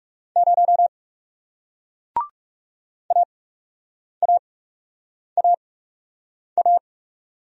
Morse code, heard as a keyed tone, about 40 wpm, spaced as for 8 wpm, 700 Hz. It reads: 0 AAAA